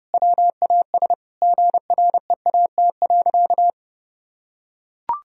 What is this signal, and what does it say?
Morse code, heard as a keyed tone, about 30 wpm, 700 Hz. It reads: WAS GREAT.
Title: Sentences from Top 1000 Words - Review Part 1 - 30wpm